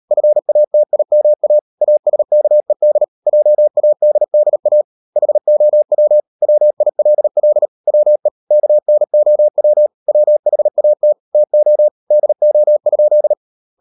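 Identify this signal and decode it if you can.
Morse code, heard as a keyed tone, about 38 wpm, 600 Hz. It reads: FATIMA ASKED JADDA HOW WILL WE KNOW WHAT TO DO?
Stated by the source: Morse code practice at 38 WPM, sending Ken Follett's novel Never in English, from CW med Gnister.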